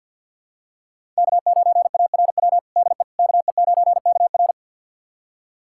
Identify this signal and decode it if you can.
Morse code, heard as a keyed tone, about 51 wpm, 700 Hz. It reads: K9ARW DE KE9KR
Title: USA Callsign DE Callsign - 5 Characters - 50wpm